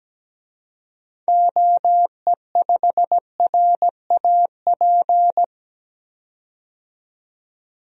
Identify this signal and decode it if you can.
Morse code, heard as a keyed tone, about 17 wpm, 700 Hz. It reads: OE5RAP
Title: DXCC-100 Call Signs - 6 Characters - 17wpm